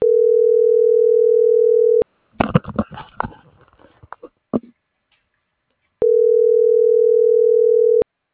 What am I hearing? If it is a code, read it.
Ringback tone (North America), 440 + 480 Hz.